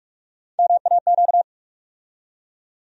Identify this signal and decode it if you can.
Morse code, heard as a keyed tone, about 43 wpm, 700 Hz. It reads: MAQ
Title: USA Call Sign Postfixes - 3 Letters 45wpm